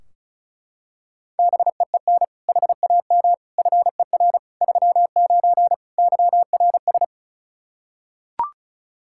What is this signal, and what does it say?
Morse code, heard as a keyed tone, about 36 wpm, 700 Hz. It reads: BEEN HAM FER 39 YRS
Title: Practice QSO - Exchange 3 - More Info - 35wpm